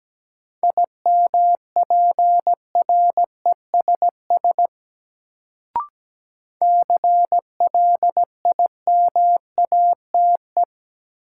Morse code, heard as a keyed tone, about 17 wpm, 700 Hz. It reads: IMPRESS CLIMATE
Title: Sentences from Top 1600 Words - New Word Drill Alt - 17wpm